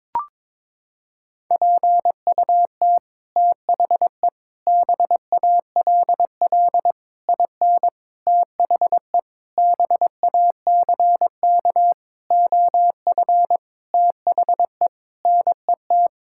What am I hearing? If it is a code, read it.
Morse code, heard as a keyed tone, about 22 wpm, 700 Hz. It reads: PUT THE BALL IN THE BACK OF THE NET
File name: Sets of 9 Words - 22wpm